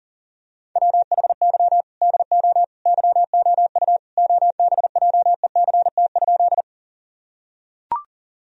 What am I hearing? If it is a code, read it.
Morse code, heard as a keyed tone, about 40 wpm, 700 Hz. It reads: WHY DO YOU OBJECT?
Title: Sentences from Top 1000 Words - Review Part 3 - 40wpm